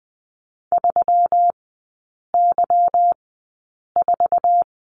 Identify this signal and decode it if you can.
Morse code, heard as a keyed tone, about 20 wpm, spaced as for 8 wpm, 690 Hz. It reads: 3Y4
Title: 159 Lesson 26e - Letter g - Three Characters 20wpm